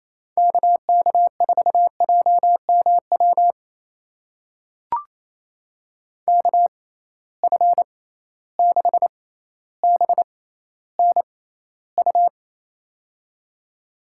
Morse code, heard as a keyed tone, about 28 wpm, spaced as for 14 wpm, 700 Hz. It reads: KK4JMW KF6BNU
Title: USA Call Signs - 6 Characters - 6x Character Spacing - 28wpm